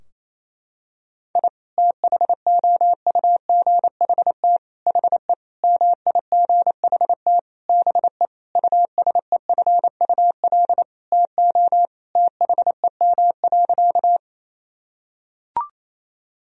Morse code, heard as a keyed tone, about 28 wpm, 700 Hz. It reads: I THOUGHT HE MIGHT BE USEFUL TO THEM.
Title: Sentences from Top 1000 Words - Review Part 4 - 28wpm